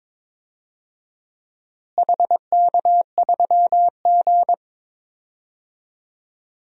Morse code, heard as a keyed tone, about 22 wpm, 700 Hz. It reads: HK3G
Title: DXCC-100 Call Signs - 4 Characters - Repeated 3x - 22wpm